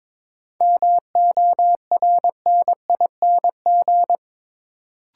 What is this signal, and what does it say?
Morse code, heard as a keyed tone, about 22 wpm, 700 Hz. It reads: MORNING
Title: Sentences from Top 400 Words - New Word Drill - 22wpm